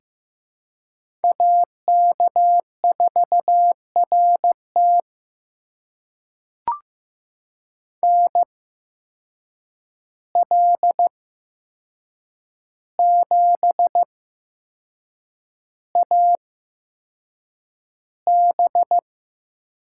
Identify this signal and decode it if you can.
Morse code, heard as a keyed tone, about 15 wpm, spaced as for 6 wpm, 700 Hz. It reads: AK4RT NL7AB
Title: USA Call Signs - 5 Characters - 8x Character Spacing - 15wpm